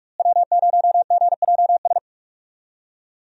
Morse code, heard as a keyed tone, about 46 wpm, 700 Hz. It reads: W0GJS
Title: USA Call Signs - 5 Characters - Repeated 3x - 45wpm